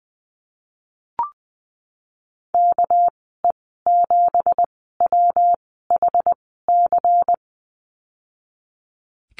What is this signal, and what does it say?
Morse code, heard as a keyed tone, about 20 wpm, spaced as for 15 wpm, 700 Hz. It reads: KE7WHC
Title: 200 Lesson 32g - Letter k - Call Signs 20wpm